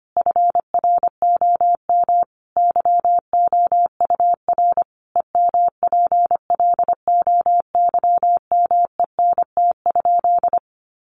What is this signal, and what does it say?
Morse code, heard as a keyed tone, about 25 wpm, 700 Hz. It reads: FROM YOUR EMPLOYMENT?
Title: Sentences from Top 1500 Words 25wpm